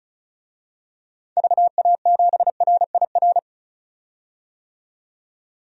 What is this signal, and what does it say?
Morse code, heard as a keyed tone, about 34 wpm, 700 Hz. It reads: VA7RIR
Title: DXCC-100 Call Signs - 6 Characters - 35wpm